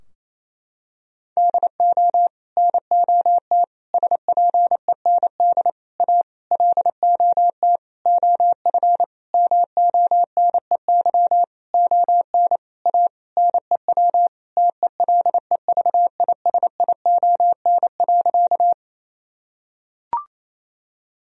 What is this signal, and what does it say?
Morse code, heard as a keyed tone, about 28 wpm, 700 Hz. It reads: DO NOT SPEND A LOT OF MONEY ON A NEW TELEVISION.
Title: Sentences from Top 1000 Words - Review Part 4 - 28wpm